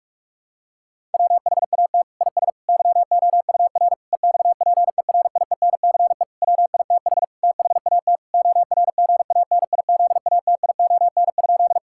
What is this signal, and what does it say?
Morse code, heard as a keyed tone, about 41 wpm, 700 Hz. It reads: WHAT IS YOUR EXPERIENCE WITH THAT ORGANIZATION?
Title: Sentences from Top 1000 Words - Review Part 5 - 45wpm